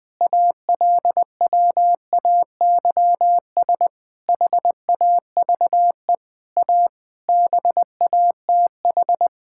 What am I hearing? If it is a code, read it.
Morse code, heard as a keyed tone, about 20 wpm, 700 Hz. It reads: ALWAYS HAVE A BATH
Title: Sentences from Top 1600 Words 20wpm